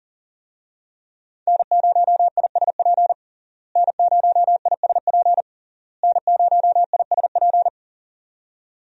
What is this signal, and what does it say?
Morse code, heard as a keyed tone, about 37 wpm, 700 Hz. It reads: N0ISP N0ISP N0ISP
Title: USA Call Signs - 5 Characters - Repeated 3x - 40wpm